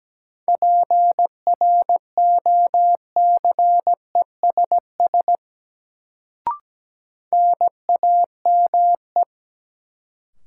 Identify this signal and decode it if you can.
Morse code, heard as a keyed tone, about 17 wpm, 700 Hz. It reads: PROCESS NAME
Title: Sets of 3 Words - New Word Drill - 17wpm